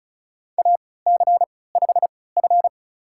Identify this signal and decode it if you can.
Morse code, heard as a keyed tone, about 36 wpm, spaced as for 20 wpm, 700 Hz. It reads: AC5F
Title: USA Call Signs - 4 Characters - 3x Character Spacing - 35wpm